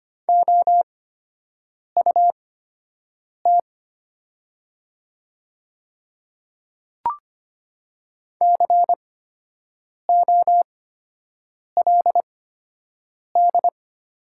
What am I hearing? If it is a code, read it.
Morse code, heard as a keyed tone, about 25 wpm, spaced as for 7 wpm, 690 Hz. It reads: OUT COLD
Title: 8x Character Spacing - Top 500 Words - 25wpm